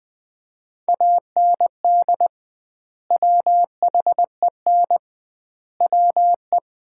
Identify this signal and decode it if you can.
Morse code, heard as a keyed tone, about 20 wpm, 700 Hz. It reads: AND WHEN WE
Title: 2x Word-Spacing - Sets of 5 Words - 20wpm